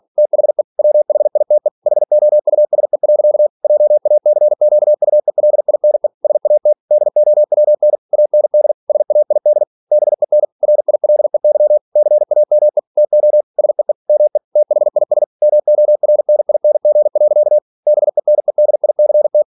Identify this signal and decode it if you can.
Morse code, heard as a keyed tone, about 46 wpm, 600 Hz. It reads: THE WHITE HOUSE. JACQUELINE SAT DOWN AND SAID BEN RILEY CAME TO SEE ME THIS MORNING. BENEDICT
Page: Never 47wpm